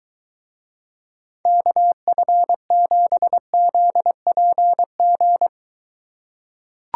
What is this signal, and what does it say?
Morse code, heard as a keyed tone, about 23 wpm, 700 Hz. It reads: KF7ZPG